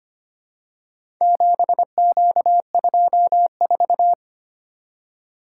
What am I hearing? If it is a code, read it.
Morse code, heard as a keyed tone, about 25 wpm, 700 Hz. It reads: 7Q24
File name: practice / 25wpm / Four Letter-Number Combinations - 25wpm